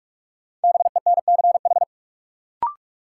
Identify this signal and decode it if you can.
Morse code, heard as a keyed tone, about 41 wpm, 700 Hz. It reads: BENCH
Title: Sentences from Top 1400 Words - New Word Drill - 45wpm